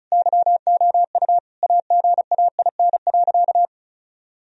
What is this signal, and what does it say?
Morse code, heard as a keyed tone, about 36 wpm, 700 Hz. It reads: YOU AGAIN.
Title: Sentences from Top 1000 Words - Review Part 1 - 35wpm